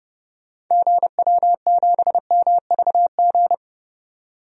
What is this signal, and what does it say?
Morse code, heard as a keyed tone, about 32 wpm, 700 Hz. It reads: GW7MVG
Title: DXCC-100 Call Signs - 6 Characters - 30wpm